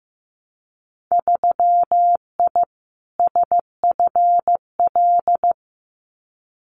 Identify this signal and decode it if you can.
Morse code, heard as a keyed tone, about 15 wpm, 700 Hz. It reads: 3I SFL
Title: Winter Field Day Exchanges 15wpm